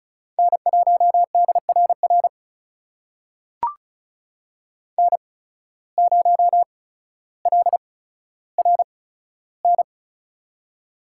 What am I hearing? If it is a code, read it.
Morse code, heard as a keyed tone, about 36 wpm, spaced as for 15 wpm, 700 Hz. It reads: N1DRR N0LRN
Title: USA Call Signs - 5 Characters - 8x Character Spacing - 35wpm